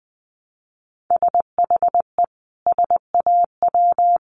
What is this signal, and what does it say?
Morse code, heard as a keyed tone, about 19 wpm, 700 Hz. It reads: SHE SAW
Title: Sentences from Top 600 Words 20wpm